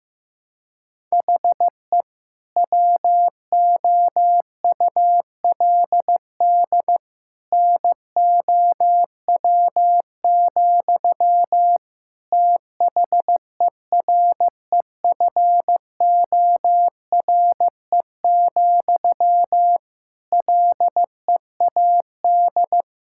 Morse code, heard as a keyed tone, about 15 wpm, 700 Hz. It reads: HE WOULD NOW, THEREFORE, LEAD
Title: Sentences from Top 1000 Words - Review Part 5 - 15wpm